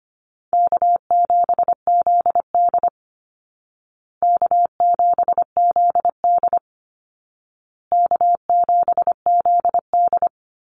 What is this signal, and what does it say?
Morse code, heard as a keyed tone, about 25 wpm, 700 Hz. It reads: K7ZD K7ZD K7ZD
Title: Easy Call Signs Repeated 3x - 25wpm